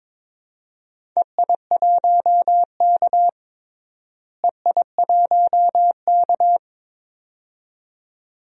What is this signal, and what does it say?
Morse code, heard as a keyed tone, about 22 wpm, 700 Hz. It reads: EI1K EI1K
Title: DXCC-100 Call Signs - 4 Characters - Repeated 2x - 22wpm